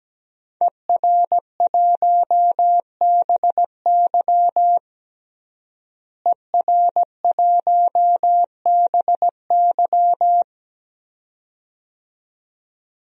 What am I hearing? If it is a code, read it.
Morse code, heard as a keyed tone, about 17 wpm, 700 Hz. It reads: ER1BY ER1BY
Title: DXCC-100 Call Signs - 5 Characters - Repeated 2x - 17wpm